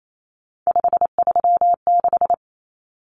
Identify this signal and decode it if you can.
Morse code, heard as a keyed tone, about 28 wpm, 700 Hz. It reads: 536
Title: Numbers - 3-Digit 28wpm